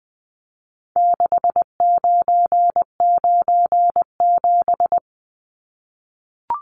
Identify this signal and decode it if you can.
Morse code, heard as a keyed tone, about 20 wpm, 700 Hz. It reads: 6997
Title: Numbers - 4-Digit 20wpm